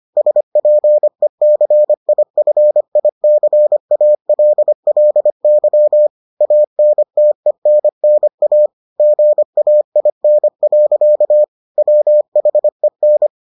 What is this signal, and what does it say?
Morse code, heard as a keyed tone, about 25 wpm, 600 Hz. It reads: SPECIFICALLY ANTENNA GAIN. WHEN